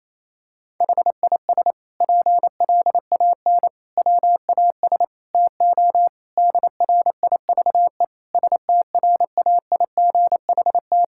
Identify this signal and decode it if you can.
Morse code, heard as a keyed tone, about 28 wpm, 700 Hz. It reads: HIS PLAN WAS TO DRIVE STRAIGHT
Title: Sentences from Top 1000 Words - Review Part 6 - 28wpm